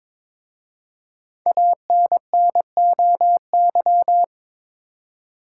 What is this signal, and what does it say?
Morse code, heard as a keyed tone, about 22 wpm, 700 Hz. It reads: ANNOY
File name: 5 Letter Words - First 3 Letters Given - 22wpm